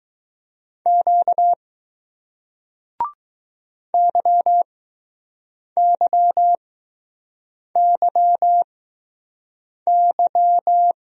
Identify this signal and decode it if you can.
Morse code, heard as a keyed tone, about 20 wpm, spaced as for 6 wpm, 700 Hz. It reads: Q YYYY